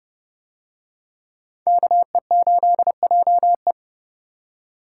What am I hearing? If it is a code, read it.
Morse code, heard as a keyed tone, about 28 wpm, 700 Hz. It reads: KE8JE